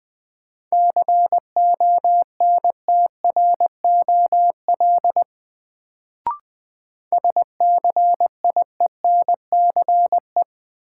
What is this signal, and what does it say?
Morse code, heard as a keyed tone, about 20 wpm, 700 Hz. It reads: CONTROL SCIENCE
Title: Sentences from Top 1000 Words - Review Part 2 - New Word Drill - 20wpm